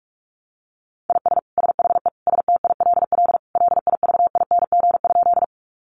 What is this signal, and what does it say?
Morse code, heard as a keyed tone, about 46 wpm, 700 Hz. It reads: IS SHE STILL LIVING?
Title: Sentences from Top 1100 Words 45wpm